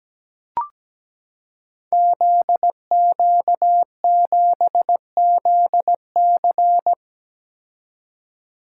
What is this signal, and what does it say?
Morse code, heard as a keyed tone, about 17 wpm, 700 Hz. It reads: ZQ7ZC
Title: Code Groups - 5 Letter:Numbers - 17wpm